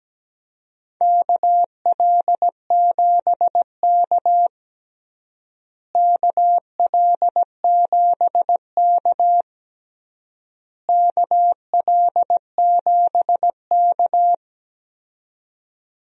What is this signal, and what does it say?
Morse code, heard as a keyed tone, about 17 wpm, 700 Hz. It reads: KL7K KL7K KL7K
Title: DXCC-100 Call Signs - 4 Characters - Repeated 3x - 17wpm